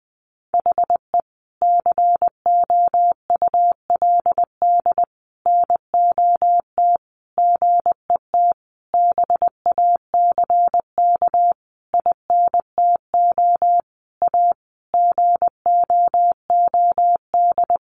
Morse code, heard as a keyed tone, about 20 wpm, 700 Hz. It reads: HE COULD NOT GET BACK INTO A GOOD
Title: Sentences from Top 1100 Words 20wpm